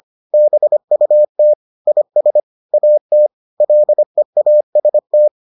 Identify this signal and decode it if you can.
Morse code, heard as a keyed tone, about 25 wpm, 600 Hz. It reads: BUT IS AT LEAST